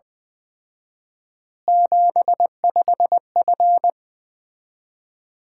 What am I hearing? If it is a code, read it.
Morse code, heard as a keyed tone, about 20 wpm, 700 Hz. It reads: 75F
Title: Common QSO Phrases - 4x Word-Spacing - 20wpm